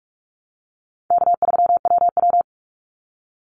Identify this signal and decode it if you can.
Morse code, heard as a keyed tone, about 46 wpm, 700 Hz. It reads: K3WW
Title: Call Signs - 45wpm